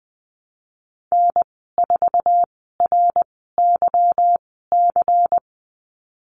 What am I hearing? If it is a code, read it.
Morse code, heard as a keyed tone, about 20 wpm, spaced as for 15 wpm, 700 Hz. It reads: N4RYC
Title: 193 Lesson 31g - Letter v - Call Signs 20wpm